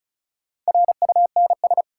Morse code, heard as a keyed tone, about 36 wpm, 700 Hz. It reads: RUNS
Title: Sentences from Top 600 Words - New Word Drill - 35wpm